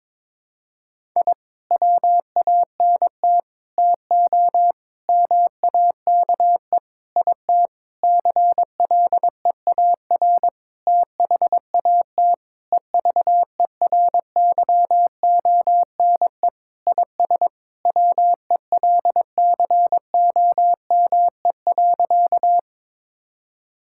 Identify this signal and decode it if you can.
Morse code, heard as a keyed tone, about 22 wpm, 700 Hz. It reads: I WANT TO MAKE IT CLEAR THAT EVERYONE IS WELCOME.
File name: Sentences with 7-Word Phrases - 22wpm